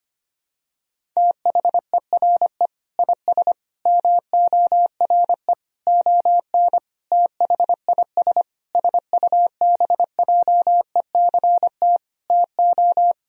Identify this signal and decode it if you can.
Morse code, heard as a keyed tone, about 25 wpm, 700 Hz. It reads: THERE IS MORE ON THIS SUBJECT TO
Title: Sentences from Top 1000 Words - Review Part 1 - 25wpm